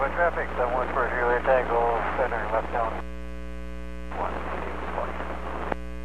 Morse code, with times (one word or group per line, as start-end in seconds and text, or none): none